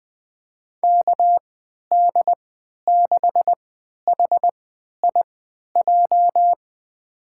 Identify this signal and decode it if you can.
Morse code, heard as a keyed tone, about 20 wpm, spaced as for 11 wpm, 700 Hz. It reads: KD6HIJ
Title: USA Call Signs - 6 Characters - 3x Character Spacing - 20wpm